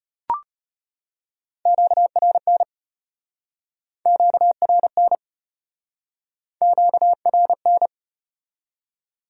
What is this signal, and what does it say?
Morse code, heard as a keyed tone, about 36 wpm, 700 Hz. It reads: QRN QRN QRN